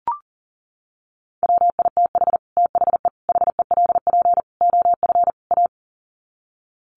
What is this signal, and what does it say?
Morse code, heard as a keyed tone, about 39 wpm, 700 Hz. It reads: WITH THE HELP OF A
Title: Sets of 5 Words 40wpm